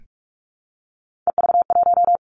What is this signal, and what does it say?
Morse code, heard as a keyed tone, about 46 wpm, 700 Hz. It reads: EV1
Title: Three Challenging Letter-Number Combinations 45wpm